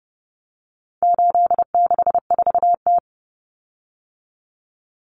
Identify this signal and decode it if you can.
Morse code, heard as a keyed tone, about 30 wpm, 700 Hz. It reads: 864T
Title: Common Cut Numbers - 4-Digit 30wpm